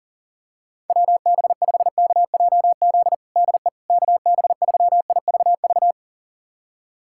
Morse code, heard as a keyed tone, about 37 wpm, 700 Hz. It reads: WB5KJZ DE KB3IVV
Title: USA Callsign DE Callsign - 6 Characters - 40wpm